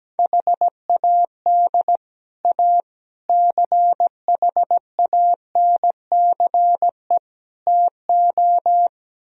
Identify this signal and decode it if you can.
Morse code, heard as a keyed tone, about 17 wpm, 700 Hz. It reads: HAD A CHANCE TO
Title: Speed-Racing - Sets of 4 Words 17wpm